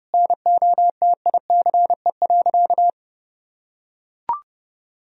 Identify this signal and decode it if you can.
Morse code, heard as a keyed tone, about 32 wpm, 700 Hz. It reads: NOTICE.
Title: Sentences from Top 1000 Words - Review Part 2 - 30wpm